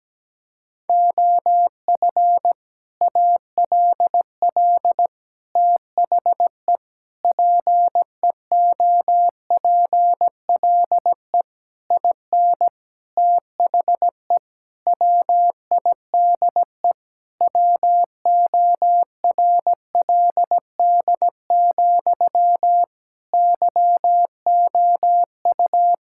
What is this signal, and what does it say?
Morse code, heard as a keyed tone, about 17 wpm, 700 Hz. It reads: OF ALL THE PEOPLE IN THE WIDE WORLD, YOU
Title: Sentences from Top 1000 Words - Review Part 6 - 17wpm